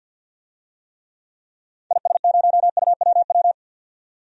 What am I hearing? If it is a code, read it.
Morse code, heard as a keyed tone, about 53 wpm, 700 Hz. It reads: IS0UWW
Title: DXCC-100 Call Signs - 6 Characters - 50wpm